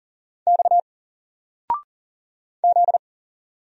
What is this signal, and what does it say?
Morse code, heard as a keyed tone, about 40 wpm, 700 Hz. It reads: X Z
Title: Single Letter-Number 40wpm